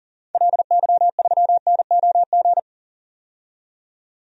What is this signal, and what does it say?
Morse code, heard as a keyed tone, about 40 wpm, 700 Hz. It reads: LY3NOG
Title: DXCC-100 Call Signs - 6 Characters - 40wpm